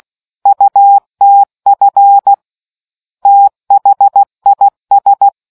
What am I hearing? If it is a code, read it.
Morse code, heard as a keyed tone, about 16 wpm, 800 Hz. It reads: UTF THIS